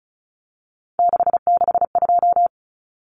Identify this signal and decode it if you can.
Morse code, heard as a keyed tone, about 34 wpm, 700 Hz. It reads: 662
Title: Numbers - 3-Digit 35wpm